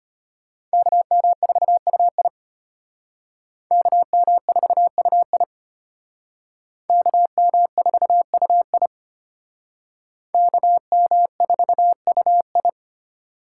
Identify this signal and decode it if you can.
Morse code, heard as a keyed tone, about 33 wpm, 700 Hz. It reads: KM4UI KM4UI KM4UI KM4UI